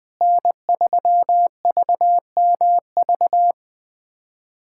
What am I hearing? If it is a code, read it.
Morse code, heard as a keyed tone, about 21 wpm, 700 Hz. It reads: N3VMV
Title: USA Callsign DE Callsign - 5 Characters - 20wpm